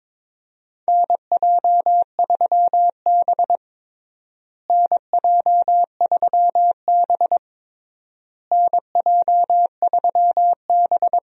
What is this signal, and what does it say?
Morse code, heard as a keyed tone, about 22 wpm, 700 Hz. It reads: NJ3B NJ3B NJ3B
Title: USA Call Signs - 4 Characters - Repeated 3x - 22wpm